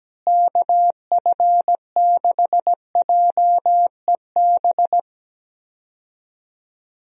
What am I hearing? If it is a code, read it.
Morse code, heard as a keyed tone, about 17 wpm, 700 Hz. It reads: KF6JEB
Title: USA Call Signs - 6 Characters - Repeated 2x - 17wpm